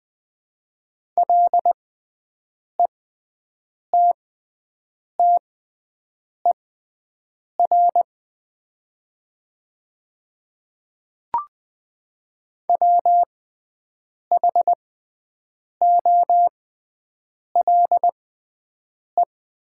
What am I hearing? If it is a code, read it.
Morse code, heard as a keyed tone, about 20 wpm, spaced as for 7 wpm, 700 Hz. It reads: LETTER WHOLE
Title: 6x Character Spacing - Top 500 Words - 20wpm